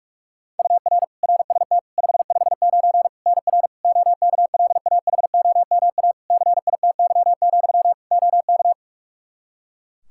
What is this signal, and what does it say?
Morse code, heard as a keyed tone, about 46 wpm, 700 Hz. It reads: UR RST 559 NR OKLAHOMA CITY, OK
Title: Practice QSO - Exchange 1 - Beginning - 45wpm